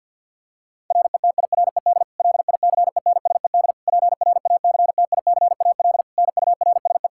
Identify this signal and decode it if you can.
Morse code, heard as a keyed tone, about 52 wpm, 700 Hz. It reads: RETIRED LICENSED PRACTICAL NURSE
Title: Practice QSO - Exchange 3 - More Info - 50wpm